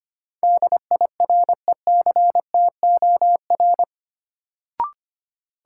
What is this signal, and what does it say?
Morse code, heard as a keyed tone, about 25 wpm, 700 Hz. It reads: DIRECTOR
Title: Sentences from Top 1300 Words - New Word Drill Alt - 25wpm